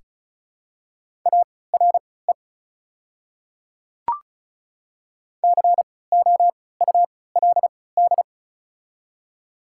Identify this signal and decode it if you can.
Morse code, heard as a keyed tone, about 34 wpm, spaced as for 20 wpm, 690 Hz. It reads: ARE COULD
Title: 3x Character Spacing - Top 100 Words - 35wpm